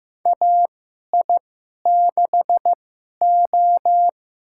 Morse code, heard as a keyed tone, about 15 wpm, spaced as for 11 wpm, 700 Hz. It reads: AI6O
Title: USA Call Signs - 4 Characters - 2x Character Spacing - 15wpm